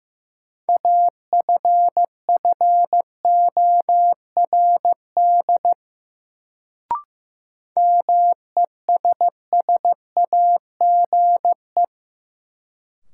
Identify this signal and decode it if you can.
Morse code, heard as a keyed tone, about 16 wpm, 700 Hz. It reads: AFFORD MESSAGE
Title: Sentences from Top 1100 Words - New Word Drill - 15wpm